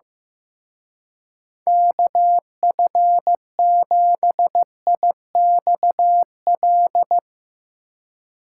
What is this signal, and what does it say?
Morse code, heard as a keyed tone, about 16 wpm, 700 Hz. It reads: KF7IXL
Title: Speed Racing - USA 2x3 Call Signs 15wpm